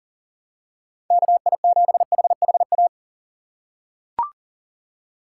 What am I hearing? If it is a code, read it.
Morse code, heard as a keyed tone, about 40 wpm, 700 Hz. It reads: KI7HHA
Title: USA Call Signs - 6 Characters - 2x Character Spacing - 40wpm